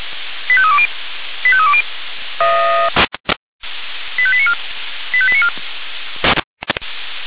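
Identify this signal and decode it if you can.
Five-tone selective call (ZVEI), heard as quick runs of five tones, tones of about 70 ms. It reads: 85320 85320 85894 85894
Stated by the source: ZVEI five-tone selective call 85310, received by SDR.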